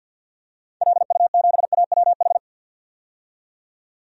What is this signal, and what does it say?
Morse code, heard as a keyed tone, about 51 wpm, 700 Hz. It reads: LU7AWH
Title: DXCC-100 Call Signs - 6 Characters - 50wpm